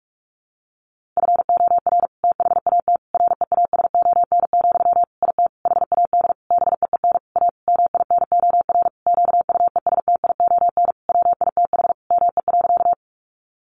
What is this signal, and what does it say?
Morse code, heard as a keyed tone, about 44 wpm, 700 Hz. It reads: FOR THAT REASON, IT HAD BEEN A MINOR QUESTION WITH ME.
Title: Sentences from Top 1500 Words 45wpm